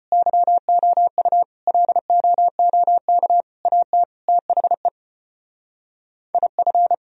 Morse code, heard as a keyed tone, about 31 wpm, 700 Hz. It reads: YOU LOOK AT THE IF
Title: Speed-Racing - Sets of 5 Words 25wpm